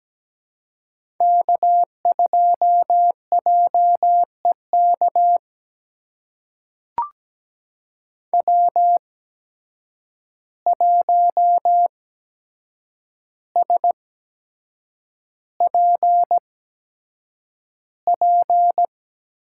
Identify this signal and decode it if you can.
Morse code, heard as a keyed tone, about 17 wpm, spaced as for 7 wpm, 700 Hz. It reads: K2JEK W1SPP
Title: USA Call Signs - 5 Characters - 8x Character Spacing - 17wpm